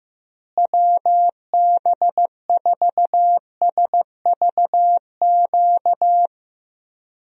Morse code, heard as a keyed tone, about 15 wpm, 700 Hz. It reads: WB4SVQ
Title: USA Call Signs - 6 Characters - Repeated 3x - 15wpm